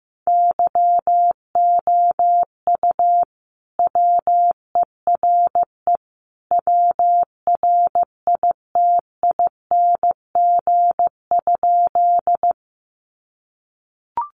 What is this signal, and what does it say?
Morse code, heard as a keyed tone, about 15 wpm, 700 Hz. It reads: YOU WERE WRITING?
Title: Sentences from Top 800 Words 15wpm